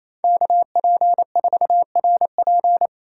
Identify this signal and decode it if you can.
Morse code, heard as a keyed tone, about 34 wpm, 700 Hz. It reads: KP4RP
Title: Speed Racing - USA 2x2 Call Signs 28wpm